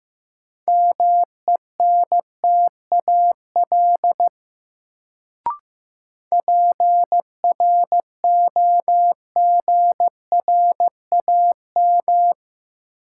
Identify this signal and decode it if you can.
Morse code, heard as a keyed tone, about 16 wpm, 700 Hz. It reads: MENTAL PROGRAM
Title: Sentences from Top 1000 Words - Review Part 4 - New Word Drill - 15wpm